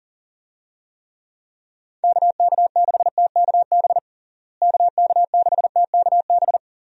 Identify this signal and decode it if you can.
Morse code, heard as a keyed tone, about 37 wpm, 700 Hz. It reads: KK6TKB KK6TKB
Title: USA Call Signs - 6 Characters - Repeated 3x - 40wpm